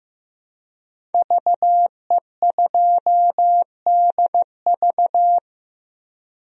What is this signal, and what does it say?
Morse code, heard as a keyed tone, about 16 wpm, 700 Hz. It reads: VE2DV
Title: DXCC-100 Call Signs - 5 Characters - Repeated 2x - 15wpm